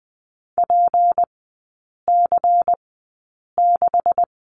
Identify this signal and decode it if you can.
Morse code, heard as a keyed tone, about 20 wpm, spaced as for 8 wpm, 700 Hz. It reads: PC6
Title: 184 Lesson 30e - Letter b - Three Characters 20wpm